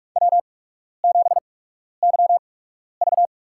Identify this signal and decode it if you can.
Morse code, heard as a keyed tone, about 44 wpm, spaced as for 12 wpm, 690 Hz. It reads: W7YV
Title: USA Call Signs - 4 Characters - 8x Character Spacing - 45wpm